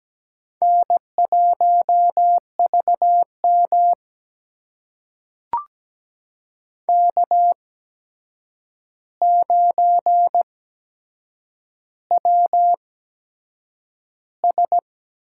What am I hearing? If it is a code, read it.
Morse code, heard as a keyed tone, about 17 wpm, spaced as for 7 wpm, 700 Hz. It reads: N1VM K9WS
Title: USA Call Signs - 4 Characters - 8x Character Spacing - 17wpm